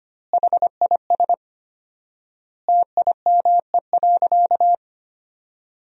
Morse code, heard as a keyed tone, about 25 wpm, 700 Hz. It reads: HIS TIME.
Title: 4x Word-Spacing - Sentences from Top 400 Words - 25wpm